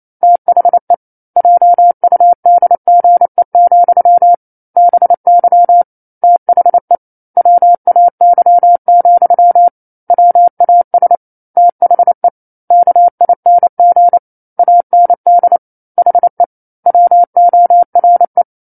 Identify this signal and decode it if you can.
Morse code, heard as a keyed tone, about 28 wpm, 700 Hz. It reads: THE JUDGE, BY THE WAY, WAS THE KING AND HE WORE
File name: 09-Chapter-09 28 WPM